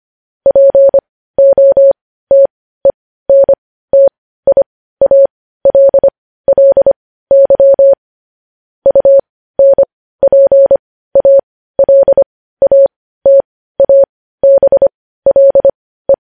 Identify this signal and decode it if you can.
Morse code, heard as a keyed tone, about 25 wpm, spaced as for 15 wpm, 550 Hz. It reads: POTENTIALLY UNPALATABLE
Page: news-headlines-in-morse-code-at-15-wpm